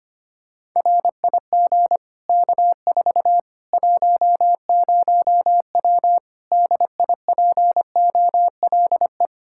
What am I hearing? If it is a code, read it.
Morse code, heard as a keyed tone, about 25 wpm, 700 Hz. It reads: RIG K4 10W DIPOLE
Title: Practice QSO - Exchange 2 - Info - 25wpm